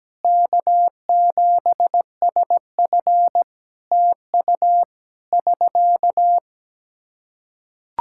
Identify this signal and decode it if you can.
Morse code, heard as a keyed tone, about 17 wpm, 700 Hz. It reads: K7SF TU <SK>
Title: Practice QSO - Exchange 4 - Ending - 17wpm